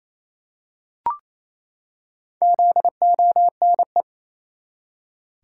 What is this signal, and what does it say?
Morse code, heard as a keyed tone, about 28 wpm, 700 Hz. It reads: ZONE
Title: Four Letter Words - 28wpm